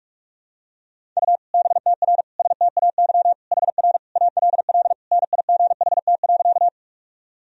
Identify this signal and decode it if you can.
Morse code, heard as a keyed tone, about 41 wpm, 700 Hz. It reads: U BTR STAY HR ALL NIGHT.
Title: Sentences from Top 500 Words and Abbreviations 45wpm